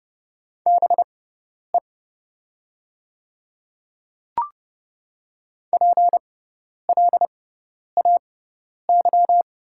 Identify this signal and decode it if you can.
Morse code, heard as a keyed tone, about 30 wpm, spaced as for 10 wpm, 700 Hz. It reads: BE PLAY
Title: 6x Character Spacing - Top 300 Words - 30wpm